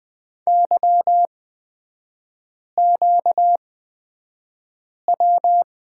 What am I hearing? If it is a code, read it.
Morse code, heard as a keyed tone, about 20 wpm, spaced as for 5 wpm, 700 Hz. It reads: YQW
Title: 243 Lesson 39e - Letter z - Three Characters 20wpm